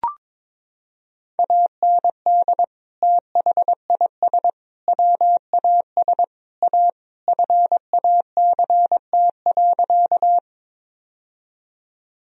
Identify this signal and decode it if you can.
Morse code, heard as a keyed tone, about 22 wpm, 700 Hz. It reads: AND THIS WAS A FACT.
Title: Sentences from Top 1000 Words - Review Part 3 - 22wpm